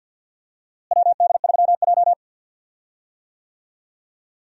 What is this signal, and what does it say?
Morse code, heard as a keyed tone, about 51 wpm, 700 Hz. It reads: WD3J
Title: USA Call Signs - 4 Characters - Repeated 2x - 50wpm